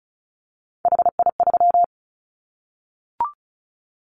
Morse code, heard as a keyed tone, about 34 wpm, 700 Hz. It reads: HI3
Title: Callsign Prefixes for Top 100 DXCC - 35wpm